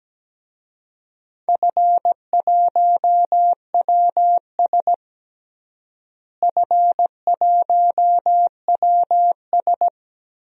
Morse code, heard as a keyed tone, about 17 wpm, 700 Hz. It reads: F1WS F1WS
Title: DXCC-100 Call Signs - 4 Characters - Repeated 3x - 17wpm